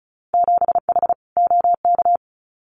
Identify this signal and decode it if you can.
Morse code, heard as a keyed tone, about 34 wpm, 700 Hz. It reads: 7H OK
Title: Winter Field Day Exchanges Encore 35wpm